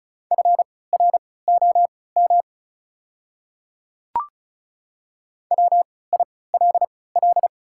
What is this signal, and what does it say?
Morse code, heard as a keyed tone, about 34 wpm, spaced as for 20 wpm, 700 Hz. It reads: FROM WILL
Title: 3x Character Spacing - Top 100 Words - 35wpm